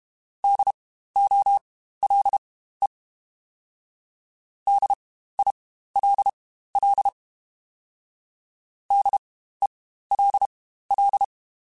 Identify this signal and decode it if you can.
Morse code, heard as a keyed tone, about 32 wpm, spaced as for 15 wpm, 800 Hz. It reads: DOLE DILL DELL
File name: Spaced Alphas / Spaced Alphas mp3s / Ds sp @30